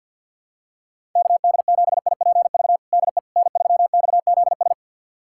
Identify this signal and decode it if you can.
Morse code, heard as a keyed tone, about 52 wpm, 700 Hz. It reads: KD7IPV DE N3XZS
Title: USA Callsign DE Callsign 50wpm